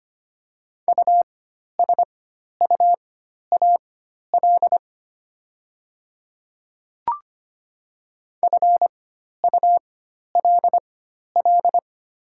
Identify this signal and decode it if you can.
Morse code, heard as a keyed tone, about 25 wpm, spaced as for 12 wpm, 700 Hz. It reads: USUAL FULL
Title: 4x Character Spacing - Top 500 Words - 25wpm